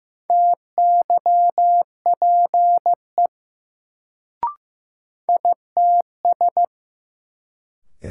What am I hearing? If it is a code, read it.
Morse code, heard as a keyed tone, about 15 wpm, 700 Hz. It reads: TYPE ITS
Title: Sentences from Top 600 Words - New Word Drill - 15wpm